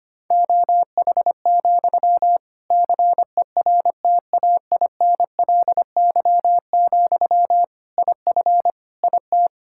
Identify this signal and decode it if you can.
Morse code, heard as a keyed tone, about 25 wpm, 700 Hz. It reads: OH, CERTAINLY, IF IT
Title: Sentences from Top 1600 Words 25wpm